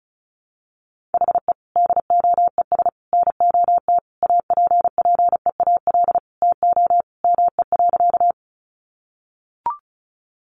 Morse code, heard as a keyed tone, about 34 wpm, 700 Hz. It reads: HE DOES NOT APPEAL TO ME.
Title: Sentences from Top 1300 Words 35wpm